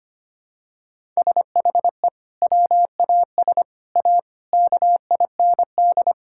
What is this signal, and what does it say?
Morse code, heard as a keyed tone, about 25 wpm, 700 Hz. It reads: SHE WAS A KIND
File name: practice / 25wpm / Sentences with 3-Word Phrases - 25wpm